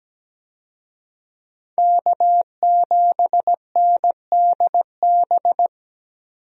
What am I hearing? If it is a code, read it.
Morse code, heard as a keyed tone, about 17 wpm, 700 Hz. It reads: K7NDB
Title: USA Call Signs - 5 Characters - 17wpm